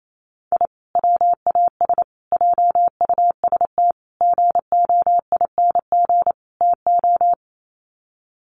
Morse code, heard as a keyed tone, about 28 wpm, 700 Hz. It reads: I WAS JUST GOING TO
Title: Sets of 5 Words Encore 28wpm